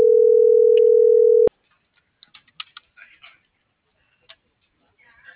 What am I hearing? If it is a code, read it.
Ringback tone (North America), 440 + 480 Hz.